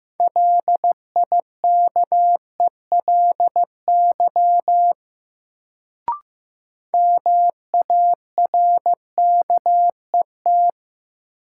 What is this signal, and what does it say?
Morse code, heard as a keyed tone, about 15 wpm, 700 Hz. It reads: LIKELY MARKET
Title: Sentences from Top 600 Words - New Word Drill - 15wpm